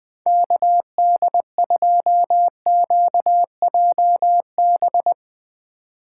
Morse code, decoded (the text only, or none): KD2QJB